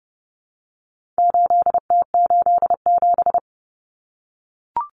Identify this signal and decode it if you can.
Morse code, heard as a keyed tone, about 30 wpm, 700 Hz. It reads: 8T87
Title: Common Cut Numbers - 4-Digit 30wpm